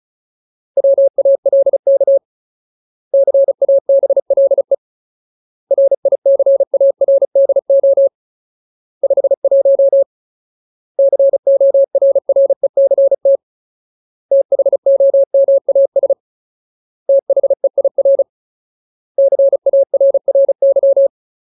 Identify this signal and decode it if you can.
Morse code, heard as a keyed tone, about 34 wpm, 550 Hz. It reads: WALK CABLE RICARDO 51 CORRECT THOMAS THEIR CARRY
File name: Ultimate Word List / 35WPM Ultimate